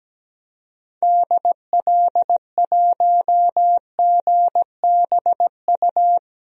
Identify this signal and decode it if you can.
Morse code, heard as a keyed tone, about 17 wpm, 700 Hz. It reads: DL1GBU
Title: DXCC-100 Call Signs - 6 Characters - 17wpm